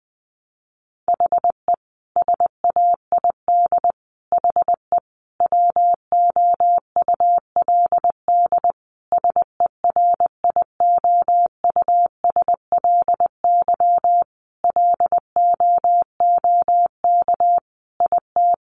Morse code, heard as a keyed tone, about 19 wpm, 700 Hz. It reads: HE SAID HE WOULD SERIOUSLY LOOK IT
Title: Sentences from Top 1200 Words 20wpm